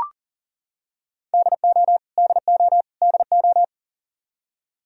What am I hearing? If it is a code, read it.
Morse code, heard as a keyed tone, about 40 wpm, 700 Hz. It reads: DO DO DO
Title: Top 100 Words Repeated 40wpm